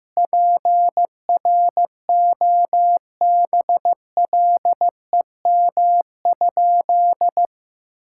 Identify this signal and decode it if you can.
Morse code, heard as a keyed tone, about 15 wpm, 700 Hz. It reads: PROBLEM?
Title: Sentences from Top 1000 Words - Review Part 6 - 15wpm